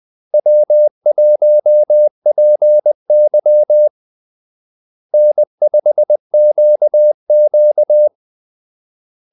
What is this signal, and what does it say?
Morse code, heard as a keyed tone, about 20 wpm, 600 Hz. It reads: W1PY N5QQ
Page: Kaldesignaler 20 wpm